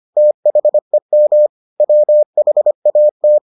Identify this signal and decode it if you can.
Morse code, heard as a keyed tone, about 25 wpm, 600 Hz. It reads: THEM WHAT